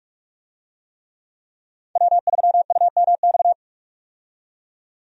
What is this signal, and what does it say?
Morse code, heard as a keyed tone, about 46 wpm, 700 Hz. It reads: W3UMX
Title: USA Call Signs - 5 Characters - 45wpm